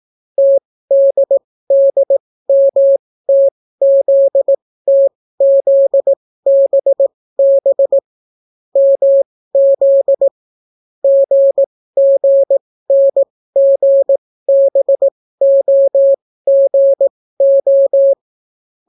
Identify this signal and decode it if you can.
Morse code, heard as a keyed tone, about 18 wpm, spaced as for 14 wpm, 550 Hz. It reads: TDDMTZTZBB MZ GGNGBOGO